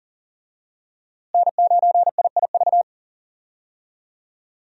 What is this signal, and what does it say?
Morse code, heard as a keyed tone, about 40 wpm, 700 Hz. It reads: N9IIV
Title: USA Call Signs - 5 Characters - Repeated 2x - 40wpm